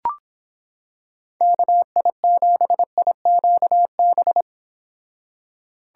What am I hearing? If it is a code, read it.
Morse code, heard as a keyed tone, about 26 wpm, 700 Hz. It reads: KI7IQB